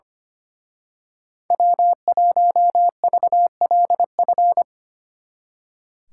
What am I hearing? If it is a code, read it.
Morse code, heard as a keyed tone, about 28 wpm, 700 Hz. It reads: W1VLF
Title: Speed Racing - USA Call Signs 25wpm